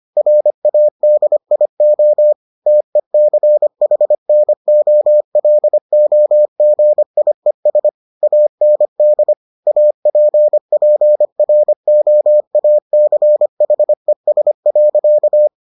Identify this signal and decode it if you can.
Morse code, heard as a keyed tone, about 25 wpm, 600 Hz. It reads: RADIO TECHNOLOGIES AND APPROACHES.